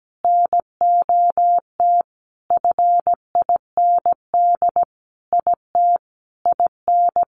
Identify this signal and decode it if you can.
Morse code, heard as a keyed tone, about 17 wpm, 700 Hz. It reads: NOT FIND IT IN
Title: Sentences from Top 1400 Words 17wpm